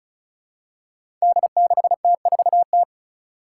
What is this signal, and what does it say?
Morse code, heard as a keyed tone, about 36 wpm, 700 Hz. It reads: D6T4T
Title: Five Challenging Letter-Number Combinations 35wpm